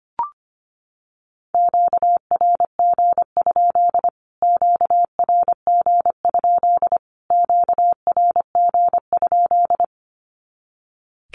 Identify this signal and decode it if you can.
Morse code, heard as a keyed tone, about 25 wpm, 700 Hz. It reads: QRG? QRG? QRG?
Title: Q-Codes Repeated 3x 25wpm